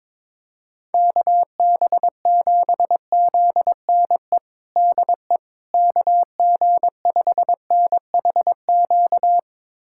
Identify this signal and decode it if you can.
Morse code, heard as a keyed tone, about 22 wpm, 700 Hz. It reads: KB7ZNE DE KG5NHQ
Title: USA Callsign DE Callsign 22wpm